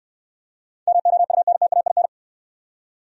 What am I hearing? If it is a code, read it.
Morse code, heard as a keyed tone, about 66 wpm, 700 Hz. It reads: MOUNTAIN